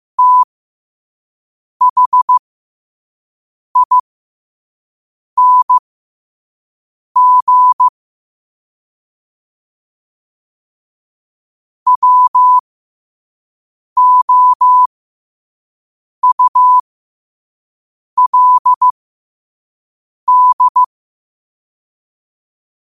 Morse code, heard as a keyed tone, about 15 wpm, spaced as for 5 wpm, 1000 Hz. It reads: THING WOULD